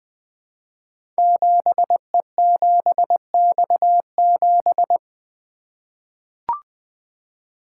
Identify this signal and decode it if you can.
Morse code, heard as a keyed tone, about 20 wpm, 700 Hz. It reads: 7E7X7
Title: Code Groups - 5 Letter:Numbers - 20wpm